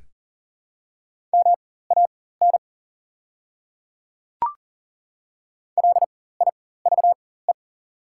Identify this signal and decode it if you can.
Morse code, heard as a keyed tone, about 40 wpm, spaced as for 19 wpm, 690 Hz. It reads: MAN LIVE
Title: 4x Character Spacing - Top 200 Words - 40wpm